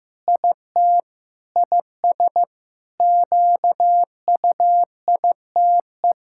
Morse code, heard as a keyed tone, about 15 wpm, 700 Hz. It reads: IT IS QUITE
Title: Sentences from Top 1000 Words - Review Part 6 - 15wpm